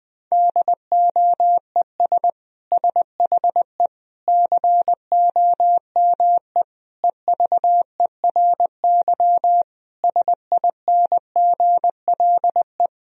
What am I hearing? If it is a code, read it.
Morse code, heard as a keyed tone, about 20 wpm, 700 Hz. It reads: DOES SHE COME EVERY SINGLE
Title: Sentences from Top 1000 Words - Review Part 4 - 20wpm